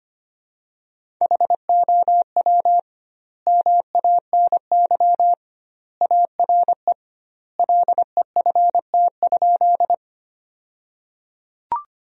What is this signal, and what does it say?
Morse code, heard as a keyed tone, about 25 wpm, 700 Hz. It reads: HOW MANY ARE LEFT?
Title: 2x Word-Spacing - Sentences from Top 300 Words - 25wpm